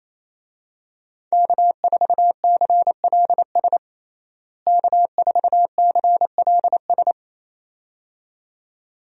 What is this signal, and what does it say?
Morse code, heard as a keyed tone, about 28 wpm, 700 Hz. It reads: K4CLS K4CLS
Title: USA Call Signs - 5 Characters - Repeated 2x - 28wpm